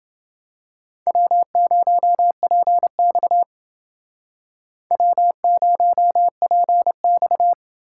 Morse code, heard as a keyed tone, about 28 wpm, 700 Hz. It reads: W0PX W0PX